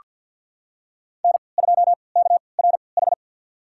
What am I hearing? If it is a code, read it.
Morse code, heard as a keyed tone, about 48 wpm, spaced as for 28 wpm, 700 Hz. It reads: N2KRH
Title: USA Call Signs - 5 Characters - 3x Character Spacing - 50wpm